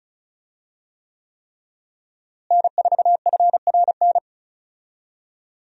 Morse code, heard as a keyed tone, about 36 wpm, 700 Hz. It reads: N4FRN
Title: USA Call Signs - 5 Characters - Repeated 3x - 35wpm